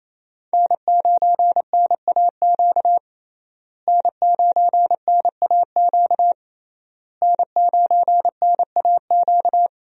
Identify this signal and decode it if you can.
Morse code, heard as a keyed tone, about 28 wpm, 700 Hz. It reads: N9NAQ N9NAQ N9NAQ
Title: USA Call Signs - 5 Characters - Repeated 3x - 28wpm